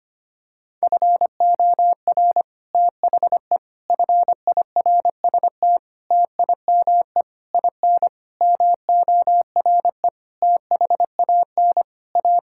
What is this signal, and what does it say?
Morse code, heard as a keyed tone, about 25 wpm, 700 Hz. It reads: FOR THE FIRST TIME IN MORE THAN A
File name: practice / 25wpm / Sentences with 9-Word Phrases - 25wpm